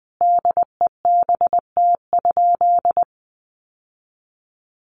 Morse code, heard as a keyed tone, about 20 wpm, 700 Hz. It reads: DEBT?
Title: Sentences from Top 1300 Words 20wpm